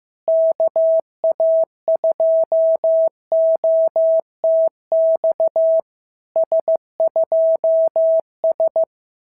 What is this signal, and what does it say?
Morse code, heard as a keyed tone, about 14 wpm, 650 Hz. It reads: KA2OTX S2S
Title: SOTA QSO Practice 15wpm